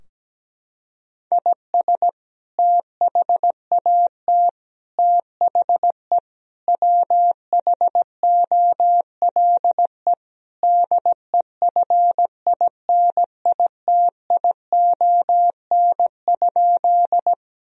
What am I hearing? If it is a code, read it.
Morse code, heard as a keyed tone, about 17 wpm, 700 Hz. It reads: IS THAT THE WHOLE DEFINITION?
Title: Sentences from Top 1000 Words - Review Part 4 - 17wpm